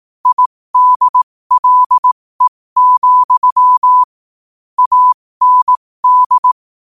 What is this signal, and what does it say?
Morse code, heard as a keyed tone, about 18 wpm, spaced as for 15 wpm, 1000 Hz. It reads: IDLE, AND
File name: qotdfiles / 15wpm